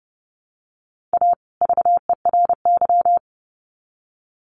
Morse code, heard as a keyed tone, about 30 wpm, 700 Hz. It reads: A VERY
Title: Sets of 2 Words 30wpm